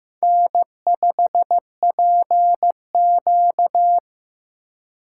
Morse code, heard as a keyed tone, about 15 wpm, 700 Hz. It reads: N5PQ
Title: USA Call Signs - 4 Characters - 3x Character Spacing - 15wpm